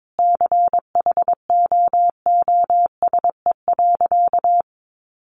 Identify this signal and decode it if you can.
Morse code, heard as a keyed tone, about 22 wpm, 700 Hz. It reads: CHOOSE.
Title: Sentences from Top 700 Words 22wpm